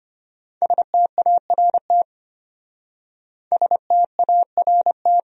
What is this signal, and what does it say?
Morse code, heard as a keyed tone, about 33 wpm, 700 Hz. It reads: START START